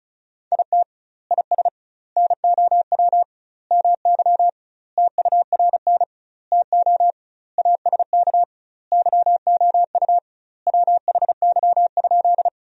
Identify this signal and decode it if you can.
Morse code, heard as a keyed tone, about 36 wpm, 700 Hz. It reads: IT IS NOW MY TURN TO ASK YOU WHY?
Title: 2x Word-Spacing - Sentences from Top 200 Words - 35wpm